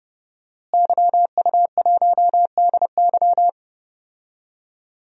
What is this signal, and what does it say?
Morse code, heard as a keyed tone, about 30 wpm, 700 Hz. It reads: YU1DY
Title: DXCC-100 Call Signs - 5 Characters - Repeated 3x - 30wpm